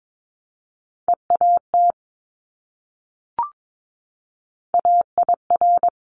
Morse code, heard as a keyed tone, about 22 wpm, 700 Hz. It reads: EAT AIR
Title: Multi-letter Phonemes Repeated 3x 22wpm